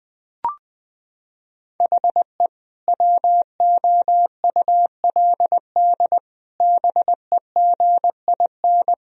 Morse code, heard as a keyed tone, about 21 wpm, 700 Hz. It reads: HE WOULD BEGIN
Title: Sentences from Top 1000 Words - Review Part 1 - 20wpm